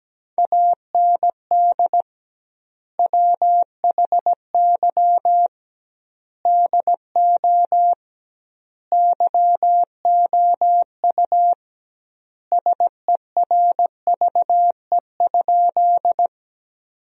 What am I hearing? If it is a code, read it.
Morse code, heard as a keyed tone, about 17 wpm, 700 Hz. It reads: AND WHY DO YOU SERVE?
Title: 2x Word-Spacing - Sentences from Top 400 Words - 17wpm